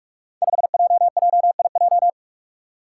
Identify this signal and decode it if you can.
Morse code, heard as a keyed tone, about 46 wpm, 700 Hz. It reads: 5JJIJ
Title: Five Challenging Letter-Number Combinations 45wpm